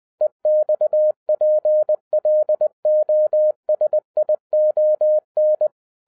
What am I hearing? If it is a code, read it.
Morse code, heard as a keyed tone, about 20 wpm, 610 Hz. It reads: EXPLOSION